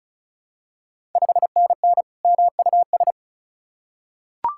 Morse code, heard as a keyed tone, about 34 wpm, 700 Hz. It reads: 5NN MUS
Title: Ontario QSO Party Exchange 35wpm